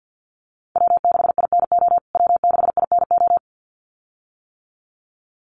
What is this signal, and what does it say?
Morse code, heard as a keyed tone, about 50 wpm, 700 Hz. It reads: W6INO W6INO
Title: USA Call Signs Repeated 2x 50wpm